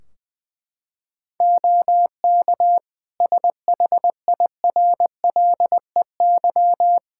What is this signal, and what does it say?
Morse code, heard as a keyed tone, about 20 wpm, 700 Hz. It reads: OK SHIRLEY
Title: Practice QSO - Exchange 3 - More Info - 20wpm